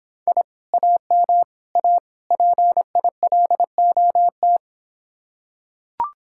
Morse code, heard as a keyed tone, about 26 wpm, 700 Hz. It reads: I AM A PILOT